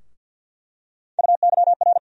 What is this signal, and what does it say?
Morse code, heard as a keyed tone, about 53 wpm, 700 Hz. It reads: UYR
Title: USA Call Sign Postfixes - 3 Letters 50wpm